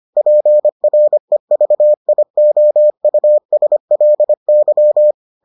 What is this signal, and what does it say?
Morse code, heard as a keyed tone, about 25 wpm, 600 Hz. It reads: PREVIOUSLY